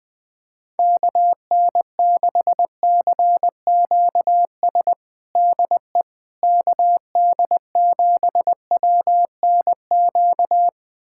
Morse code, heard as a keyed tone, about 20 wpm, 700 Hz. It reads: KN6CQS DE KD7WNQ
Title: USA Callsign DE Callsign - 6 Characters - 20wpm